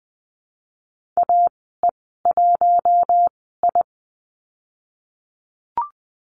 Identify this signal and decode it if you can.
Morse code, heard as a keyed tone, about 20 wpm, spaced as for 15 wpm, 700 Hz. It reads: AE1I
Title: 042 Lesson 8g - Number 1 - Call Signs 20wpm